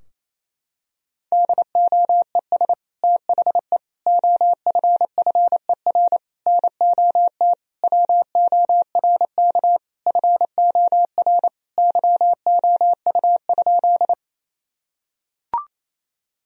Morse code, heard as a keyed tone, about 28 wpm, 700 Hz. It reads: DOES THE OFFER NOT WORK FOR YOU?
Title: Sentences from Top 1000 Words - Review Part 4 - 28wpm